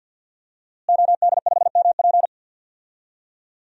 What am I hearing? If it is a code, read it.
Morse code, heard as a keyed tone, about 53 wpm, 700 Hz. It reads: OD5MP
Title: DXCC-100 Call Signs - 5 Characters - Repeated 2x - 50wpm